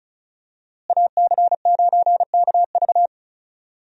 Morse code, heard as a keyed tone, about 36 wpm, 700 Hz. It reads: AC9KV
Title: USA Call Signs - 5 Characters - Repeated 2x - 35wpm